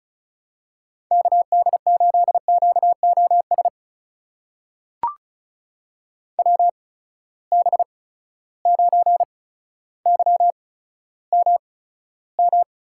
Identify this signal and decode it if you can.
Morse code, heard as a keyed tone, about 36 wpm, spaced as for 15 wpm, 700 Hz. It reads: KD8QOS WB9YMM
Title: USA Call Signs - 6 Characters - 8x Character Spacing - 35wpm